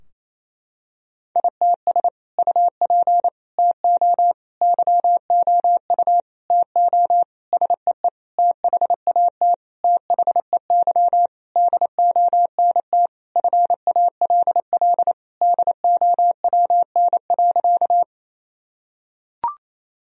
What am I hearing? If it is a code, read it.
Morse code, heard as a keyed tone, about 28 wpm, 700 Hz. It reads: ITS UP TO YOU TO SEE THAT THEY DONT FALL DOWN.
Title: Sentences from Top 1000 Words - Review Part 2 - 28wpm